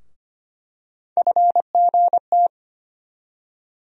Morse code, heard as a keyed tone, about 25 wpm, 700 Hz. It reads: FGT
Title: USA Call Sign Postfixes - 3 Letters 25wpm